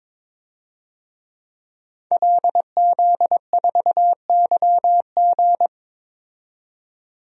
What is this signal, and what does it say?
Morse code, heard as a keyed tone, about 22 wpm, 700 Hz. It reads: LZ4YG
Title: DXCC-100 Call Signs - 5 Characters - 22wpm